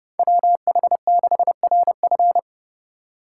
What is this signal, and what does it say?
Morse code, heard as a keyed tone, about 28 wpm, 700 Hz. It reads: WH6RF